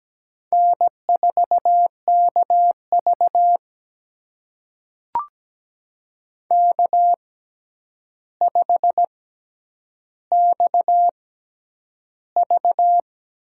Morse code, heard as a keyed tone, about 17 wpm, spaced as for 9 wpm, 700 Hz. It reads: N4KV K5XV
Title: USA Call Signs - 4 Characters - 6x Character Spacing - 17wpm